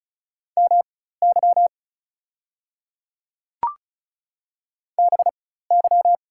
Morse code, heard as a keyed tone, about 34 wpm, spaced as for 16 wpm, 700 Hz. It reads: MY BY
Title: 4x Character Spacing - Top 200 Words - 35wpm